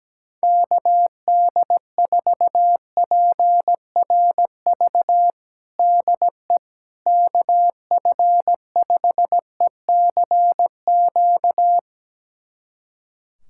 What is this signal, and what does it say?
Morse code, heard as a keyed tone, about 17 wpm, 700 Hz. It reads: KD4PRV DE KF5ECQ
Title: USA Callsign DE Callsign - 6 Characters - 17wpm